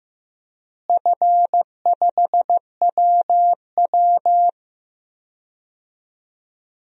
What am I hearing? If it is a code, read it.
Morse code, heard as a keyed tone, about 16 wpm, 700 Hz. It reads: F5WW
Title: DXCC-100 Call Signs - 4 Characters - Repeated 3x - 15wpm